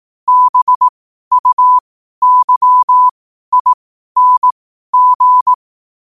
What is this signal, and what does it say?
Morse code, heard as a keyed tone, about 18 wpm, spaced as for 12 wpm, 1000 Hz. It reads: BUYING
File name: qotdfiles / 12wpm